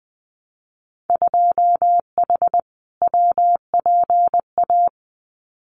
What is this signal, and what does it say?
Morse code, heard as a keyed tone, about 20 wpm, 700 Hz. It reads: 2H WPA
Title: Winter Field Day Exchanges 20wpm